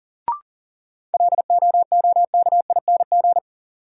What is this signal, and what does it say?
Morse code, heard as a keyed tone, about 39 wpm, 700 Hz. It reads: LOOKING
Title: Sentences from Top 1000 Words - Review Part 6 - New Word Drill - 40wpm